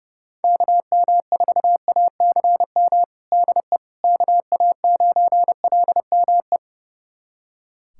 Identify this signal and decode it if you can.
Morse code, heard as a keyed tone, about 30 wpm, 700 Hz. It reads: KM4ACM DE KA9LME
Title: USA Callsign DE Callsign 30wpm